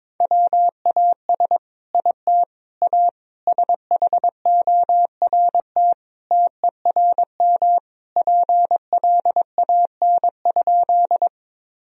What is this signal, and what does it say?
Morse code, heard as a keyed tone, about 21 wpm, 700 Hz. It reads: WAS IT A SHORT TERM PLAN?
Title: Sentences from Top 1000 Words - Review Part 4 - 22wpm